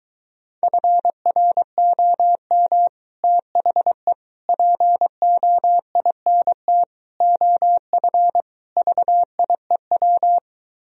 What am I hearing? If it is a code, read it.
Morse code, heard as a keyed tone, about 23 wpm, 700 Hz. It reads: FROM THE POINT OF VIEW